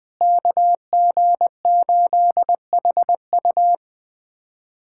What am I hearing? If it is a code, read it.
Morse code, heard as a keyed tone, about 21 wpm, 700 Hz. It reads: KG8HU
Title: USA Callsign DE Callsign 20wpm